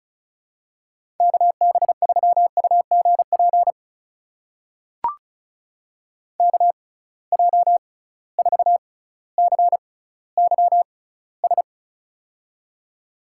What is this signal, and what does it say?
Morse code, heard as a keyed tone, about 36 wpm, spaced as for 18 wpm, 700 Hz. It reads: KB3UGP KJ4CYS
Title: USA Call Signs - 6 Characters - 6x Character Spacing - 35wpm